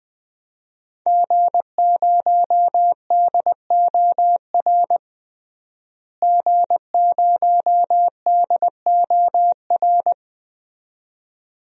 Morse code, heard as a keyed tone, about 21 wpm, 700 Hz. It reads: G0DOR G0DOR
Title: DXCC-100 Call Signs - 5 Characters - Repeated 2x - 20wpm